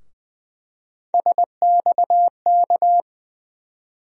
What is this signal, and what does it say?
Morse code, heard as a keyed tone, about 20 wpm, 700 Hz. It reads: SXK
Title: USA Call Sign Postfixes - 3 Letters 20wpm